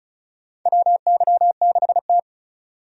Morse code, heard as a keyed tone, about 36 wpm, 700 Hz. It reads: WY6T
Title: USA Call Signs - 4 Characters - 35wpm